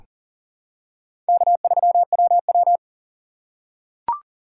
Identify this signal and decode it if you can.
Morse code, heard as a keyed tone, about 38 wpm, 700 Hz. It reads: K3WW
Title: USA Call Signs - 3x Character Spacing - 40wpm